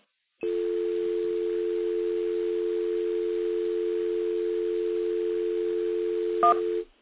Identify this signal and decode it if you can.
Dial tone (North America), 350 + 440 Hz.